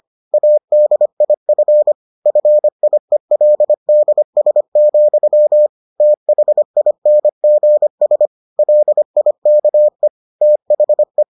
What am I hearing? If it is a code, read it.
Morse code, heard as a keyed tone, about 25 wpm, 600 Hz. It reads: ADIF FIELDS, THINGS LIKE THE